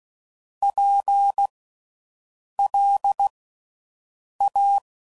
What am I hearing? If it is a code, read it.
Morse code, heard as a keyed tone, about 16 wpm, 800 Hz. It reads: P L A
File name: Suffixes / General / Houses